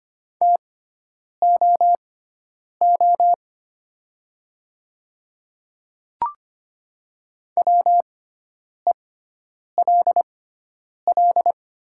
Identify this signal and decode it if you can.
Morse code, heard as a keyed tone, about 25 wpm, spaced as for 9 wpm, 700 Hz. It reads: TOO WELL
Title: 6x Character Spacing - Top 500 Words - 25wpm